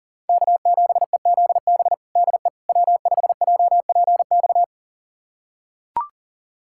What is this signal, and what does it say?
Morse code, heard as a keyed tone, about 40 wpm, 700 Hz. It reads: K7EZB DE W5JPX
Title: USA Callsign DE Callsign - 5 Characters - 40wpm